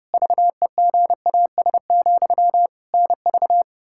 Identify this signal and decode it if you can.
Morse code, heard as a keyed tone, about 32 wpm, 700 Hz. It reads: VEGAS, NV
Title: Practice QSO - Exchange 1 - Beginning - 30wpm